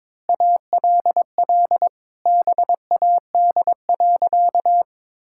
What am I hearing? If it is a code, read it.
Morse code, heard as a keyed tone, about 22 wpm, 700 Hz. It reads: ALL BAD.
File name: practice / 22wpm / Sentences with 2-Word Phrases - 22wpm